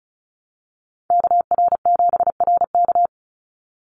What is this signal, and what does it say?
Morse code, heard as a keyed tone, about 34 wpm, 700 Hz. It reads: KR7RK
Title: NA SOTA Chaser Callsigns 35wpm